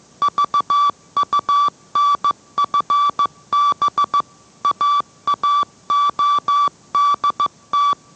Morse code, heard as a keyed tone, about 15 wpm, 1160 Hz. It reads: VUNFB AAODT